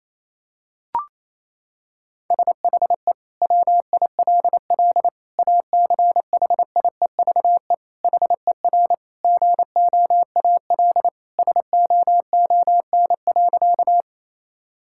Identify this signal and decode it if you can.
Morse code, heard as a keyed tone, about 28 wpm, 700 Hz. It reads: SHE WILL ACHIEVE HER GOAL SOON.
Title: Sentences from Top 1000 Words - Review Part 6 - 28wpm